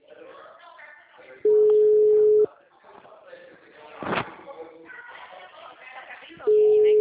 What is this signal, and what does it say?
Ringback tone (Europe, 425 Hz), with speech.